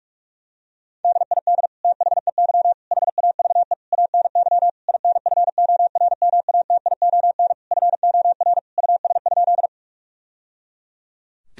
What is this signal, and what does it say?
Morse code, heard as a keyed tone, about 46 wpm, 700 Hz. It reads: DID THEY HAVE ANY INFORMATION FOR US?
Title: Sentences from Top 1000 Words - Review Part 3 - 45wpm